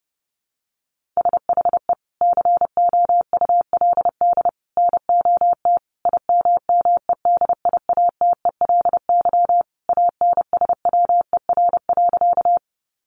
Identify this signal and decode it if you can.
Morse code, heard as a keyed tone, about 30 wpm, 700 Hz. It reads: SHE COULD NOT IMMEDIATELY ANSWER.
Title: Sentences from Top 1100 Words 30wpm